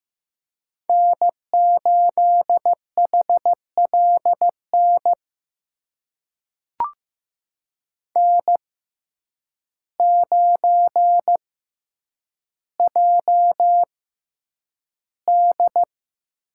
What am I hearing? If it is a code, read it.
Morse code, heard as a keyed tone, about 15 wpm, spaced as for 8 wpm, 700 Hz. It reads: N8HLN N9JD
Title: USA Call Signs - 5 Characters - 6x Character Spacing - 15wpm